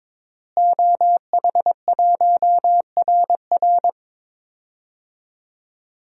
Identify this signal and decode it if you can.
Morse code, heard as a keyed tone, about 22 wpm, 700 Hz. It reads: OH1RR
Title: DXCC-100 Call Signs - 5 Characters - Repeated 3x - 22wpm